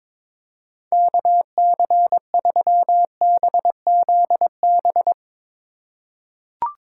Morse code, heard as a keyed tone, about 22 wpm, 700 Hz. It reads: KC3BZB
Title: USA Call Signs - 6 Characters - 4x Character Spacing - 22wpm